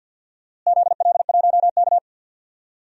Morse code, heard as a keyed tone, about 46 wpm, 700 Hz. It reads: ZL1K
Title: DXCC-100 Call Signs - 4 Characters - Repeated 2x - 50wpm